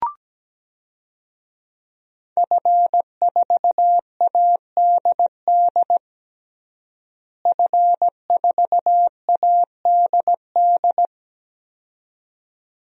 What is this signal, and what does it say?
Morse code, heard as a keyed tone, about 17 wpm, 700 Hz. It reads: F4ADD F4ADD
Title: DXCC-100 Call Signs - 5 Characters - Repeated 2x - 17wpm